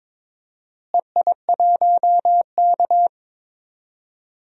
Morse code, heard as a keyed tone, about 21 wpm, 700 Hz. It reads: EI1K
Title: DXCC-100 Call Signs - 4 Characters - Repeated 3x - 22wpm